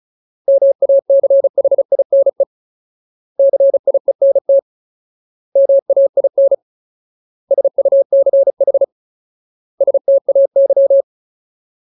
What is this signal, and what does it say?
Morse code, heard as a keyed tone, about 34 wpm, 550 Hz. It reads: MACHINE CIENT MAIN SUCH STAY
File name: Ultimate Word List / 35WPM Ultimate